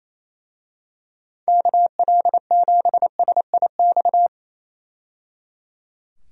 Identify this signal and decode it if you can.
Morse code, heard as a keyed tone, about 28 wpm, 700 Hz. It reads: KL7SIX
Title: DXCC-100 Call Signs - 6 Characters - 28wpm